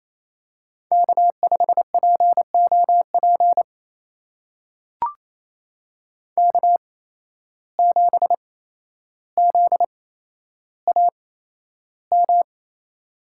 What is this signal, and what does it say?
Morse code, heard as a keyed tone, about 28 wpm, spaced as for 12 wpm, 700 Hz. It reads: K5POP K7ZAM
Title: USA Call Signs - 5 Characters - 8x Character Spacing - 28wpm